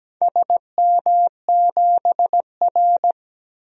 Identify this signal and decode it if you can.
Morse code, heard as a keyed tone, about 17 wpm, 700 Hz. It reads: SM7R
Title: DXCC-100 Call Signs - 4 Characters - Repeated 2x - 17wpm